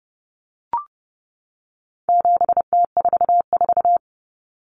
Morse code, heard as a keyed tone, about 30 wpm, 700 Hz. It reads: 7T44
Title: Common Cut Numbers - 4-Digit 30wpm